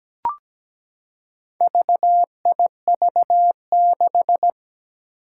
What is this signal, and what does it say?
Morse code, heard as a keyed tone, about 17 wpm, 700 Hz. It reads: VIV6
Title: Four Challenging Letter-Number Combinations 17wpm